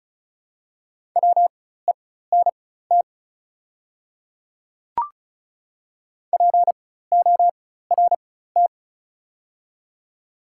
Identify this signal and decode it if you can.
Morse code, heard as a keyed tone, about 36 wpm, spaced as for 16 wpm, 700 Hz. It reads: WENT PORT
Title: 4x Character Spacing - Top 200 Words - 35wpm